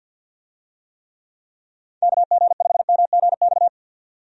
Morse code, heard as a keyed tone, about 50 wpm, 700 Hz. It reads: KG5MGX
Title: USA Call Signs - 6 Characters - 50wpm